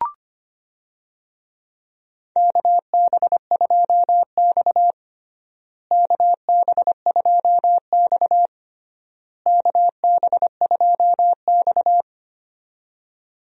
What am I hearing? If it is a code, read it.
Morse code, heard as a keyed tone, about 25 wpm, 700 Hz. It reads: KB2X KB2X KB2X
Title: USA Call Signs - 4 Characters - Repeated 3x - 25wpm